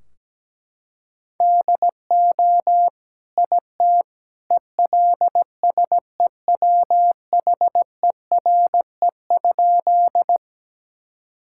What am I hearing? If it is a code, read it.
Morse code, heard as a keyed tone, about 17 wpm, 700 Hz. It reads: DO IT ELSEWHERE?
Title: Sentences from Top 1600 Words 17wpm